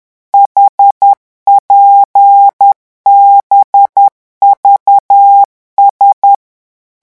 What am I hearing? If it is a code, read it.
Morse code, heard as a keyed tone, about 10 wpm, 800 Hz. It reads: HPBVS